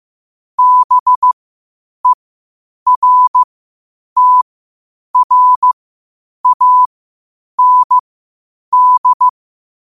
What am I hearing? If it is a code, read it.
Morse code, heard as a keyed tone, about 15 wpm, spaced as for 8 wpm, 1000 Hz. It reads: BERTRAND
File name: qotdfiles / 8wpm